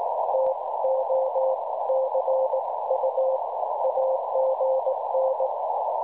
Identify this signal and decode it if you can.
Morse code, heard as a keyed tone, about 19 wpm, 530 Hz, with noise. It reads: TO CU AGN
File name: cwronde / websdr recording .6kHz